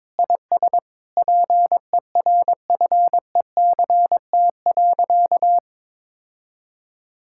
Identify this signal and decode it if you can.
Morse code, heard as a keyed tone, about 22 wpm, 700 Hz. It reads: IS PERFECT.
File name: Sentences with 9-Word Phrases - 22wpm